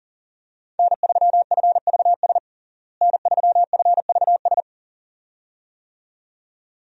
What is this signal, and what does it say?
Morse code, heard as a keyed tone, about 37 wpm, 700 Hz. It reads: N3FVS N3FVS
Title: USA Call Signs - 5 Characters - Repeated 2x - 40wpm